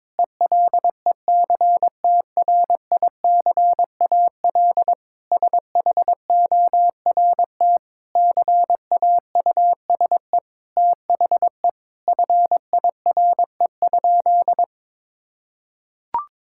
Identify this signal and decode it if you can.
Morse code, heard as a keyed tone, about 22 wpm, 700 Hz. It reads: ELECTRICAL SHORT CAUSE THE FIRE?
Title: Sentences from Top 1000 Words - Review Part 6 - 22wpm